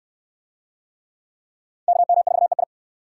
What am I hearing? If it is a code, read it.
Morse code, heard as a keyed tone, about 66 wpm, 700 Hz. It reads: KM4EI